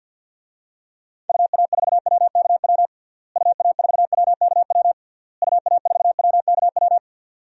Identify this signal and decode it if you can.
Morse code, heard as a keyed tone, about 47 wpm, 700 Hz. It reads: UA4WKW UA4WKW UA4WKW
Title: DXCC-100 Call Signs - 6 Characters - Repeated 3x - 50wpm